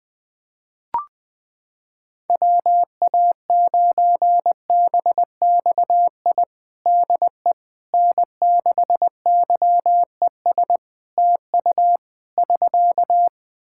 Morse code, heard as a keyed tone, about 20 wpm, 700 Hz. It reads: WA9BXI DE N6YES TU <SK>
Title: Practice QSO - Exchange 4 - Ending - 20wpm